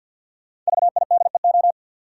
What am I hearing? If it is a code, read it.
Morse code, heard as a keyed tone, about 50 wpm, 700 Hz. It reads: VIDEO
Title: Sentences from Top 1000 Words - Review Part 4 - New Word Drill - 50wpm